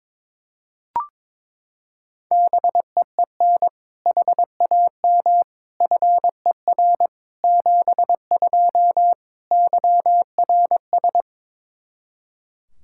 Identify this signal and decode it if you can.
Morse code, heard as a keyed tone, about 22 wpm, 700 Hz. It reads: BEEN HAM FER 72 YRS
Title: Practice QSO - Exchange 3 - More Info - 22wpm